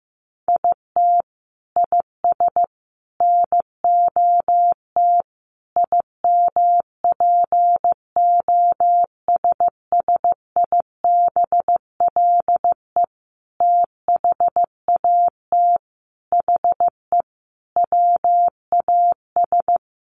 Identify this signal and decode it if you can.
Morse code, heard as a keyed tone, about 15 wpm, 700 Hz. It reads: IT IS NOT IMPOSSIBLE THAT HE WAS
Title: Sentences from Top 1100 Words 15wpm